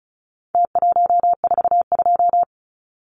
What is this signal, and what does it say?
Morse code, heard as a keyed tone, about 34 wpm, 700 Hz. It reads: T142
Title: Numbers - Contest Style - 4-Digit 35wpm